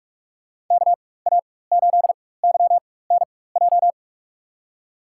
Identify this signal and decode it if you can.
Morse code, heard as a keyed tone, about 46 wpm, spaced as for 21 wpm, 700 Hz. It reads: KA8YNJ
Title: USA Call Signs - 6 Characters - 4x Character Spacing - 45wpm